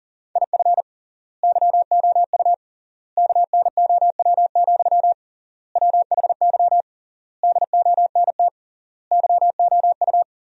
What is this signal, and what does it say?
Morse code, heard as a keyed tone, about 40 wpm, 700 Hz. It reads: IF YOU KNOW, WHY DONT YOU
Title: 3x Word-Spacing - Sentences from Top 300 Words - 40wpm